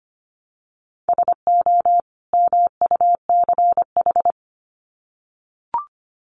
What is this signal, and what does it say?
Morse code, heard as a keyed tone, about 25 wpm, 700 Hz. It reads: SO MUCH
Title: Sets of 2 Words 25wpm